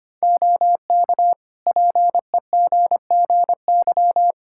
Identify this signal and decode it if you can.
Morse code, heard as a keyed tone, about 25 wpm, 700 Hz. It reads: OK PEGGY
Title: Practice QSO - Exchange 3 - More Info - 25wpm